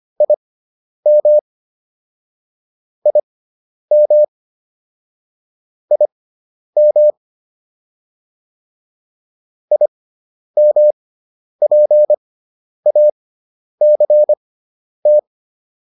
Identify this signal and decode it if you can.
Morse code, heard as a keyed tone, about 25 wpm, spaced as for 10 wpm, 600 Hz. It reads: IM IM IM IMPACT
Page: intermediate-practice-files